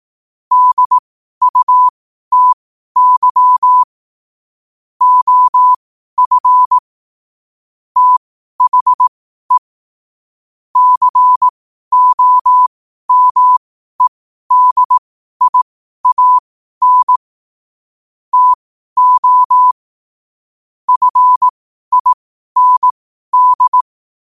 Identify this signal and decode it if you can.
Morse code, heard as a keyed tone, about 18 wpm, spaced as for 12 wpm, 1000 Hz. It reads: DUTY OF THE COMEDIAN TO FIND
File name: qotdfiles / 12wpm